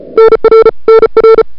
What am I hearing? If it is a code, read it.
Morse code, heard as a keyed tone, about 34 wpm, 1340 Hz, with noise. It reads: NR NR
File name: n6tr / Fab2007CWSprintHour1